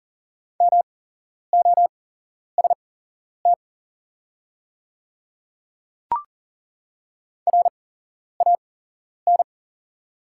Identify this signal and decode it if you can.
Morse code, heard as a keyed tone, about 38 wpm, spaced as for 11 wpm, 700 Hz. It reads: MOST RAN
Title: 8x Character Spacing - Top 500 Words - 40wpm